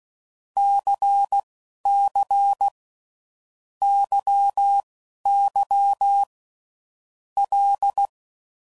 Morse code, heard as a keyed tone, about 16 wpm, spaced as for 11 wpm, 800 Hz. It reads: CC YY L
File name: Groups / Basic Letter Groups / LFCYQ